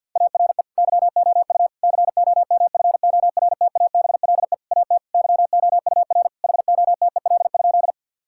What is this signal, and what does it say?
Morse code, heard as a keyed tone, about 50 wpm, 700 Hz. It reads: ARE YOU COMFORTABLE AT YOUR HOTEL?
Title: Sentences from Top 1600 Words 50wpm